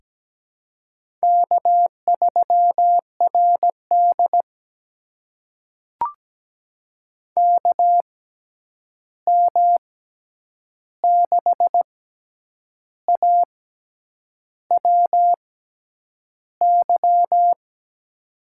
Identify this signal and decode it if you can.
Morse code, heard as a keyed tone, about 17 wpm, spaced as for 8 wpm, 700 Hz. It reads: K3RD KM6AWY
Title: USA Call Signs - 6x Character Spacing - 17wpm